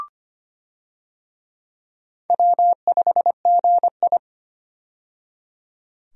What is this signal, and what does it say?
Morse code, heard as a keyed tone, about 25 wpm, 700 Hz. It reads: W5GI
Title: USA Call Signs - 4 Characters - 25wpm